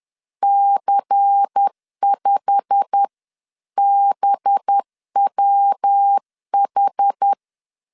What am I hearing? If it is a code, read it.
Morse code, heard as a keyed tone, about 10 wpm, 800 Hz. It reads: C5BWH
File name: LEZIONI / 9 2867 10wpm 2733